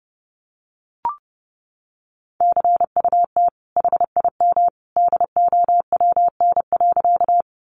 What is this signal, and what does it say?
Morse code, heard as a keyed tone, about 30 wpm, 700 Hz. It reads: CUT HIM DOWN.
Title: Sentences from Top 300 Words 30wpm